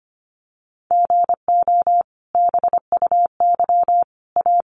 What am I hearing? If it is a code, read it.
Morse code, heard as a keyed tone, about 25 wpm, 700 Hz. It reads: GO BUY A
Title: Sentences from Top 1500 Words 25wpm